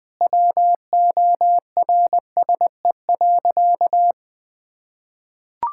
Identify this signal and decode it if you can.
Morse code, heard as a keyed tone, about 20 wpm, 700 Hz. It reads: WORSE.
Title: Sentences with 3-Word Phrases - 20wpm